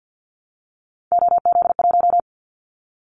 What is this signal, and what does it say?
Morse code, heard as a keyed tone, about 50 wpm, 700 Hz. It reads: OZ1
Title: Callsign Prefixes for Top 50 DXCC - 50wpm